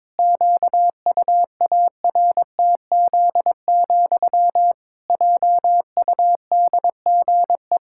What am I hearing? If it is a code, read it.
Morse code, heard as a keyed tone, about 21 wpm, 700 Hz. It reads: QUARTZ, JUDGE
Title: Pangram Challenge v2 - 22wpm